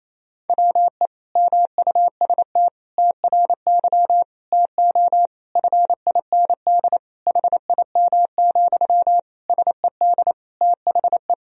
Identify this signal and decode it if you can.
Morse code, heard as a keyed tone, about 28 wpm, 700 Hz. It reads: WE MUST TRY TO FIND HIM, SED THE
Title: Sentences from Top 400 Words and Abbreviations 28wpm